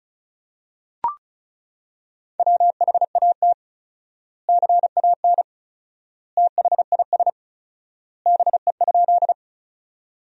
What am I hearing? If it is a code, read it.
Morse code, heard as a keyed tone, about 34 wpm, 700 Hz. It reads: WHAT CAN THIS BE?
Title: 4x Word-Spacing - Sentences from Top 100 Words - 35wpm